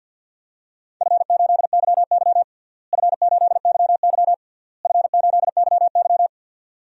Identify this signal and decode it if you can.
Morse code, heard as a keyed tone, about 51 wpm, 700 Hz. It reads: F8YY F8YY F8YY
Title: DXCC-100 Call Signs - 4 Characters - Repeated 3x - 50wpm